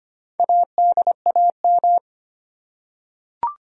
Morse code, heard as a keyed tone, about 25 wpm, 700 Hz. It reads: ADAM
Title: USA Common Names - Max 4 Letters - 25wpm